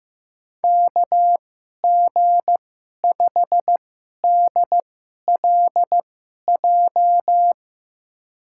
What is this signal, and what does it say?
Morse code, heard as a keyed tone, about 15 wpm, spaced as for 11 wpm, 700 Hz. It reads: KG5DLJ
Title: USA Call Signs - 2x Character Spacing - 15wpm